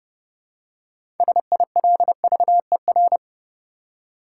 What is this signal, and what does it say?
Morse code, heard as a keyed tone, about 29 wpm, 700 Hz. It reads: SILVER
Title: Six Letter Words - 30wpm